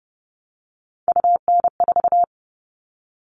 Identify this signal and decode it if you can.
Morse code, heard as a keyed tone, about 30 wpm, 700 Hz. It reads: UN4
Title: All Cut Numbers - 3-Digit 30wpm